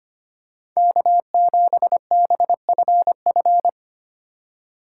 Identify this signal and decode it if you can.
Morse code, heard as a keyed tone, about 25 wpm, 700 Hz. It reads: K7BFF
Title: USA Call Signs - 5 Characters - 2x Character Spacing - 25wpm